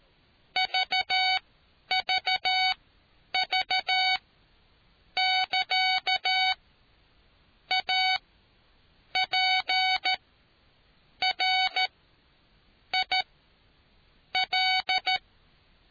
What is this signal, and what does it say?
Morse code, heard as a keyed tone, about 13 wpm, spaced as for 7 wpm, 750 Hz. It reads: VVV <KA> APRIL